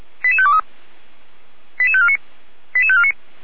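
Five-tone selective call (ZVEI), heard as quick runs of five tones, tones of about 70 ms. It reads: 88523 88549 88549
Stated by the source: ZVEI five-tone selective call, code 88507, received by SDR.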